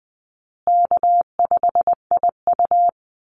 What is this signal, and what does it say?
Morse code, heard as a keyed tone, about 19 wpm, 700 Hz. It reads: K5IU
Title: Easy Call Signs Repeated 2x - 20wpm